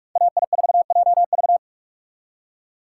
Morse code, heard as a keyed tone, about 43 wpm, 700 Hz. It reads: AI4JV